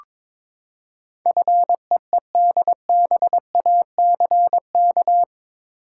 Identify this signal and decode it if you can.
Morse code, heard as a keyed tone, about 22 wpm, 700 Hz. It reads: FEEDBACK
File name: practice / 22wpm / Eight Letter Words - 22wpm